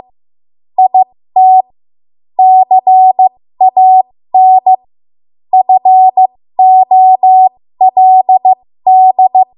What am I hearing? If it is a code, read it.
Morse code, heard as a keyed tone, about 15 wpm, spaced as for 13 wpm, 750 Hz. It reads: IT CAN FOLD
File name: Morse / 13WPM